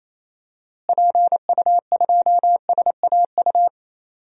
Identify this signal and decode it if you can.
Morse code, heard as a keyed tone, about 28 wpm, 700 Hz. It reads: PU2SAU
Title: DXCC-100 Call Signs - 6 Characters - Repeated 2x - 28wpm